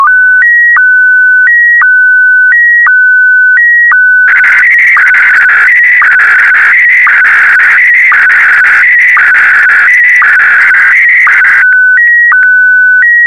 SSTV (slow-scan television), Scottie DX.